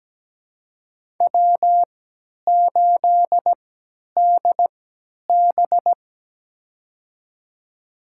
Morse code, heard as a keyed tone, about 17 wpm, spaced as for 10 wpm, 700 Hz. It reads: W8DB
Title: USA Call Signs - 4 Characters - 3x Character Spacing - 17wpm